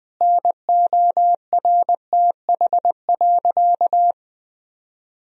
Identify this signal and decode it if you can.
Morse code, heard as a keyed tone, about 21 wpm, 700 Hz. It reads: NORTH.
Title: Sentences from Top 1000 Words - Review Part 1 - 20wpm